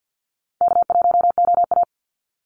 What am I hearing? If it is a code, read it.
Morse code, heard as a keyed tone, about 50 wpm, 700 Hz. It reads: K1OA
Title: Easy Call Signs - 50wpm